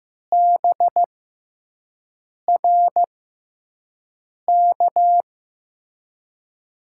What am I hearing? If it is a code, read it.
Morse code, heard as a keyed tone, about 15 wpm, spaced as for 5 wpm, 700 Hz. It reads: BRK
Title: USA Call Signs - 6x Character Spacing - 15wpm